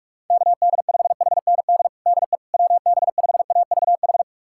Morse code, heard as a keyed tone, about 46 wpm, 700 Hz. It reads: KD5HND DE WB5AVH
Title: USA Callsign DE Callsign - 6 Characters - 45wpm